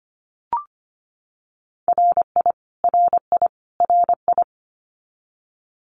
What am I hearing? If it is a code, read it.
Morse code, heard as a keyed tone, about 25 wpm, 700 Hz. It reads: RI RI RI
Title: US States Repeated 3x 25wpm